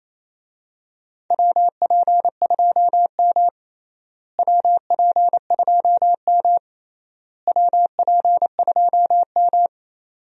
Morse code, heard as a keyed tone, about 28 wpm, 700 Hz. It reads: WP2M WP2M WP2M
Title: USA Call Signs - 4 Characters - Repeated 3x - 28wpm